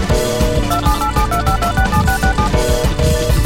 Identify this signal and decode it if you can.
DTMF, heard as telephone touch tones, tones of about 110 ms. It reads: 278725267667